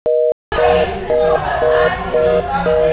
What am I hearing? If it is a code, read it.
Reorder tone (North America), 480 + 620 Hz, with speech.